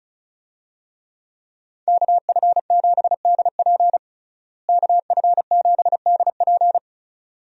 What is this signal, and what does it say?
Morse code, heard as a keyed tone, about 36 wpm, 700 Hz. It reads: KF7DP KF7DP
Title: USA Call Signs - 5 Characters - Repeated 2x - 35wpm